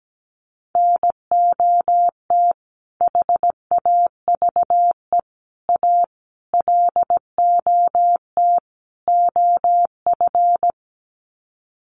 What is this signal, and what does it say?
Morse code, heard as a keyed tone, about 17 wpm, 700 Hz. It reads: NOT HAVE A LOT OF
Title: Sets of 5 Words 17wpm